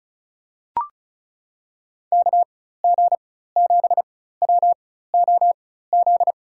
Morse code, heard as a keyed tone, about 34 wpm, spaced as for 16 wpm, 700 Hz. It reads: KG7WOZ
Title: USA Call Signs - 6 Characters - 4x Character Spacing - 35wpm